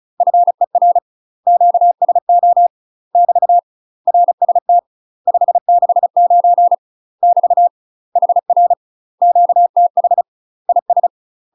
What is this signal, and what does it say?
Morse code, heard as a keyed tone, about 36 wpm, 700 Hz. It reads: FER QSO <BT> RST 569 <BT> HR QTH IS
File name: UK MASTER QSOs TRAINING / 35WPM / UK MASTER QSOs 0028